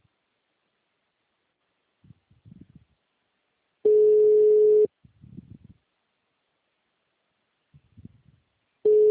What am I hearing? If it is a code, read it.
Ringback tone (Europe, 425 Hz).
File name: out- -2011- - - .19180